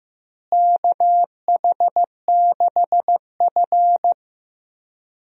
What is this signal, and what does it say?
Morse code, heard as a keyed tone, about 15 wpm, 700 Hz. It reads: KH6F
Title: DXCC-100 Call Signs - 4 Characters - Repeated 2x - 15wpm